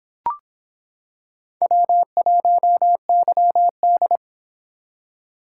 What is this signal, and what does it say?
Morse code, heard as a keyed tone, about 25 wpm, 700 Hz. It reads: W1YD